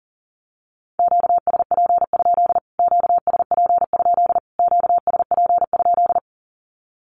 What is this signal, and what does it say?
Morse code, heard as a keyed tone, about 40 wpm, 700 Hz. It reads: QSP? QSP? QSP?
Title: Q-Codes Repeated 3x 40wpm